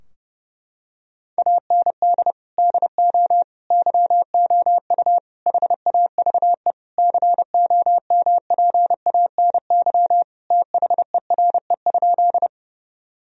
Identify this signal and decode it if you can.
Morse code, heard as a keyed tone, about 30 wpm, 700 Hz. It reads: AND DO YOU HAVE COMPANY THERE?
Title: Sentences from Top 1000 Words - Review Part 3 - 30wpm